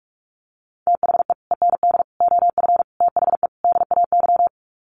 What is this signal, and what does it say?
Morse code, heard as a keyed tone, about 44 wpm, 700 Hz. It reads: THE END OF THE DAY
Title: Sets of 5 Words 45wpm